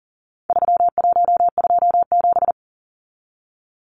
Morse code, heard as a keyed tone, about 40 wpm, 700 Hz. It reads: 3127
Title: Numbers - 4-Digit 40wpm